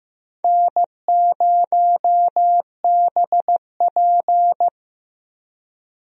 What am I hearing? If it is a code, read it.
Morse code, heard as a keyed tone, about 15 wpm, 700 Hz. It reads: N0BP
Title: USA Call Signs - 4 Characters - Repeated 3x - 15wpm